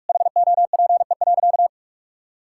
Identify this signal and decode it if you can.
Morse code, heard as a keyed tone, about 46 wpm, 700 Hz. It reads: HOPE.
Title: Sentences from Top 1000 Words - Review Part 6 - 45wpm